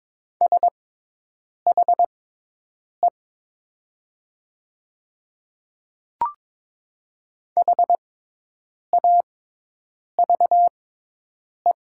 Morse code, heard as a keyed tone, about 22 wpm, spaced as for 8 wpm, 700 Hz. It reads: SHE HAVE
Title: 6x Character Spacing - Top 300 Words - 22wpm